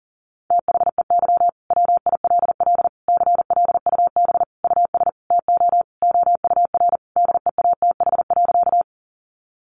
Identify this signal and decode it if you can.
Morse code, heard as a keyed tone, about 37 wpm, 700 Hz. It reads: THEY WILL CLUB US TO OUR DEATH.
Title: Sentences from Top 1200 Words 40wpm